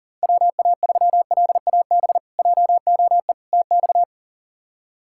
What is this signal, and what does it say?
Morse code, heard as a keyed tone, about 40 wpm, 700 Hz. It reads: WA3LAB JOE TX
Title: K1USN SST QSO Practice - 40wpm